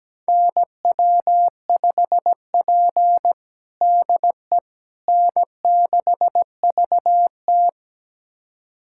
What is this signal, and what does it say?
Morse code, heard as a keyed tone, about 17 wpm, 700 Hz. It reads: NW5P DE N6VT
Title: USA Callsign DE Callsign - 4 Characters - 17wpm